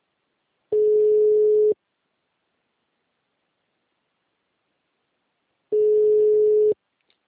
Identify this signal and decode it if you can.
Ringback tone (Europe, 425 Hz).